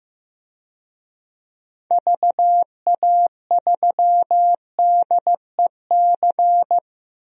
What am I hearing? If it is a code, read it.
Morse code, heard as a keyed tone, about 15 wpm, 700 Hz. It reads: VA3DEC
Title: DXCC-100 Call Signs - 6 Characters - 15wpm